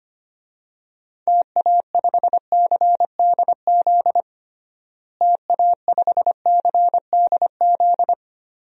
Morse code, heard as a keyed tone, about 25 wpm, 700 Hz. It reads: TA5CDZ TA5CDZ
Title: DXCC-100 Call Signs - 6 Characters - Repeated 2x - 25wpm